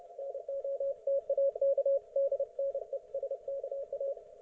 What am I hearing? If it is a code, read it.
Morse code, heard as a keyed tone, about 30 wpm, 550 Hz, with radio noise. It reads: NOTAK DNESKA
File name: Archiv / CUC181 / ok1fhi cul181